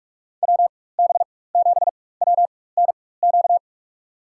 Morse code, heard as a keyed tone, about 43 wpm, spaced as for 21 wpm, 700 Hz. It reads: WB7WNQ
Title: USA Call Signs - 6 Characters - 4x Character Spacing - 45wpm